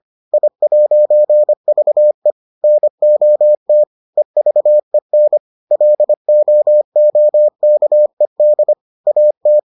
Morse code, heard as a keyed tone, about 25 wpm, 600 Hz. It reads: I'VE NOT EVEN LOOKED AT